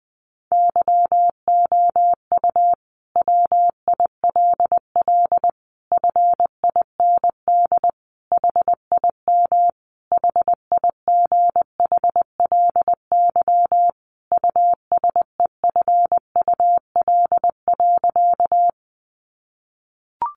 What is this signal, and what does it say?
Morse code, heard as a keyed tone, about 20 wpm, 700 Hz. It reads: YOU WILL FIND HIM HIGHLY USEFUL.
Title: Sentences from Top 1100 Words 20wpm